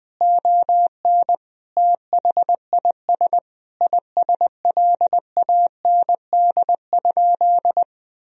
Morse code, heard as a keyed tone, about 20 wpm, 700 Hz. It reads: ON THIS ISLAND?
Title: Sentences from Top 1000 Words - Review Part 3 - 20wpm